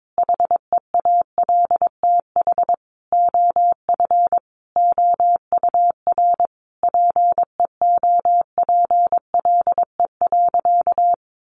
Morse code, heard as a keyed tone, about 22 wpm, 700 Hz. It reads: HEALTH OF OUR PEOPLE.
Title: Sentences from Top 1500 Words 22wpm